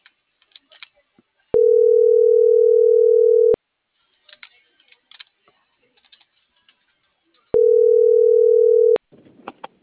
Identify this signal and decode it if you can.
Ringback tone (North America), 440 + 480 Hz.